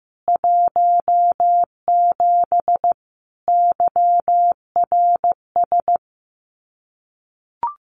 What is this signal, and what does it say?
Morse code, heard as a keyed tone, about 15 wpm, 700 Hz. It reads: 17 YRS
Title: Common QSO Phrases Encore 15wpm